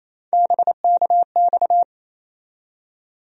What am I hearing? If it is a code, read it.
Morse code, heard as a keyed tone, about 28 wpm, 700 Hz. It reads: BKX
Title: USA Call Sign Postfixes - 3 Letters 28wpm